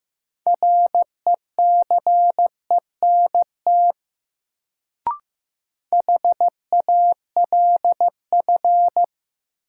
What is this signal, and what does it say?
Morse code, heard as a keyed tone, about 15 wpm, 700 Hz. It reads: RECENT HALF
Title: Sets of 3 Words - New Word Drill - 15wpm